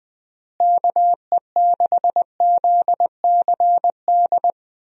Morse code, heard as a keyed tone, about 20 wpm, 700 Hz. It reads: KE6ZCD
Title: USA Call Signs - 6 Characters - Repeated 2x - 20wpm